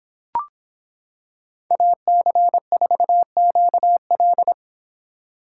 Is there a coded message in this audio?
Yes: Morse code from 1.7 to 4.5 s, heard as a keyed tone, about 25 wpm, 700 Hz. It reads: AC4QL